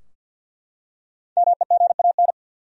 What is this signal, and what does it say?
Morse code, heard as a keyed tone, about 48 wpm, 700 Hz. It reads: MEGAN
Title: USA Common Names - 50wpm